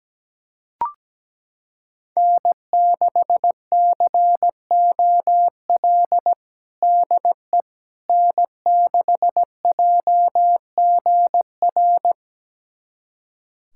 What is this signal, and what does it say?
Morse code, heard as a keyed tone, about 17 wpm, 700 Hz. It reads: N6COL DE N6JGR
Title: USA Callsign DE Callsign - 5 Characters - 17wpm